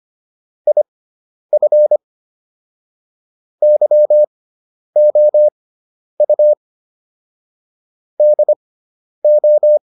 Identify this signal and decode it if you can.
Morse code, heard as a keyed tone, about 25 wpm, spaced as for 10 wpm, 600 Hz. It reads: IF YOU DO